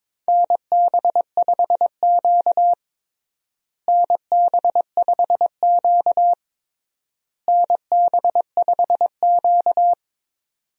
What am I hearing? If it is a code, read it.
Morse code, heard as a keyed tone, about 22 wpm, 700 Hz. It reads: NB5Q NB5Q NB5Q
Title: USA Call Signs - 4 Characters - Repeated 3x - 22wpm